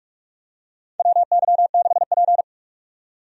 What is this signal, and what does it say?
Morse code, heard as a keyed tone, about 42 wpm, 700 Hz. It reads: WY6P
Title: USA Call Signs - 4 Characters - 45wpm